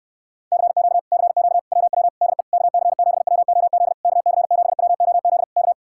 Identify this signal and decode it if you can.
Morse code, heard as a keyed tone, about 73 wpm, 700 Hz. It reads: CQ CQ WW DE KC7WOZ KC7WOZ K